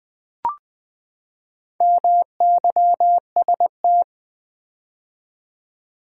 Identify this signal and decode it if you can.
Morse code, heard as a keyed tone, about 20 wpm, 700 Hz. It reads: MYST
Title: Build A Word - 4 Letters At A Time - 10-Letter Words - 20wpm